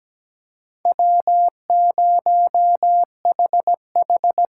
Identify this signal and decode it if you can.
Morse code, heard as a keyed tone, about 17 wpm, 700 Hz. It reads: W0HH
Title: USA Call Signs - 4 Characters - 4x Character Spacing - 17wpm